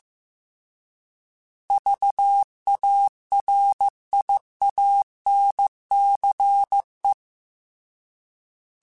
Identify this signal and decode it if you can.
Morse code, heard as a keyed tone, about 15 wpm, 800 Hz. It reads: VARIANCE
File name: Alphabetical / Alphabetical mp3s / Vs @13